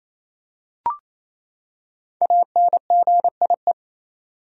Morse code, heard as a keyed tone, about 28 wpm, 700 Hz. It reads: ANGIE
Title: USA Common Names - Max 5 Letters - 28wpm